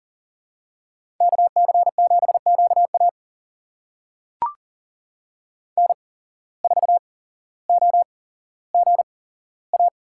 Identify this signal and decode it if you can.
Morse code, heard as a keyed tone, about 40 wpm, spaced as for 17 wpm, 700 Hz. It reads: KC7QA N4OGA
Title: USA Call Signs - 5 Characters - 8x Character Spacing - 40wpm